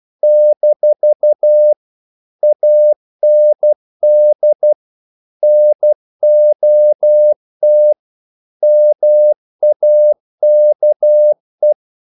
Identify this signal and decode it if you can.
Morse code, heard as a keyed tone, about 12 wpm, 600 Hz. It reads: - AND NOT MAKE
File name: CW / Practice / If12